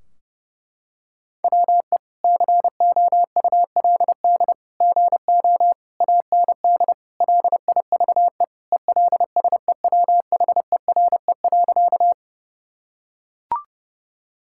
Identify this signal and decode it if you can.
Morse code, heard as a keyed tone, about 32 wpm, 700 Hz. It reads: WE COULD GO AND LIVE ELSEWHERE.
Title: Sentences from Top 1600 Words 30wpm